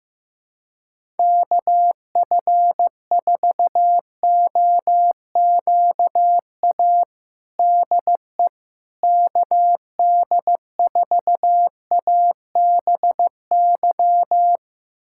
Morse code, heard as a keyed tone, about 15 wpm, 700 Hz. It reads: KF4OQA DE KD4ABY
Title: USA Callsign DE Callsign 15wpm